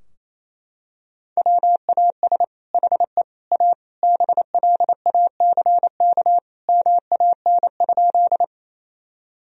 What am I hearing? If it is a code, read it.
Morse code, heard as a keyed tone, about 28 wpm, 700 Hz. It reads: WAS HE A BLACK MAN?
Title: Sentences from Top 1000 Words - Review Part 2 - 28wpm